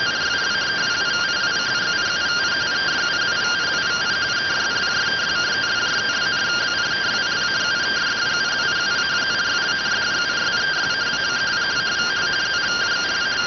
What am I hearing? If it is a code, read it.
Two-tone teleprinter signal, about 170 Hz shift.